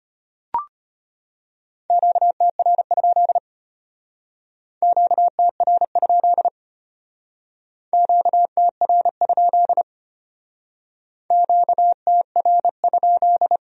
Morse code, heard as a keyed tone, about 31 wpm, 700 Hz. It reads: QTR? QTR? QTR? QTR?
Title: Speed-Racing Q-Codes - 25wpm